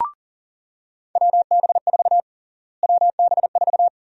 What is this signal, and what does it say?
Morse code, heard as a keyed tone, about 40 wpm, 700 Hz. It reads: WB4 WB4
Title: USA Call Sign Prefixes - Repeated 2x - 40wpm